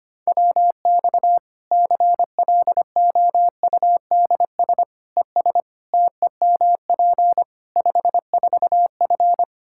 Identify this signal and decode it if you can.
Morse code, heard as a keyed tone, about 25 wpm, 700 Hz. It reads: WX CLOUDS ES TEMP 54F
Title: Practice QSO - Exchange 2 - Info - 25wpm